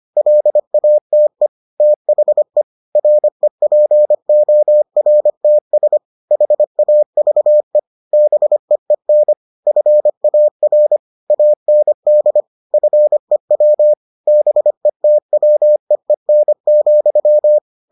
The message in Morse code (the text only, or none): LATE THE REPORTS HAVE BEEN FAR AND FEW BETWEEN,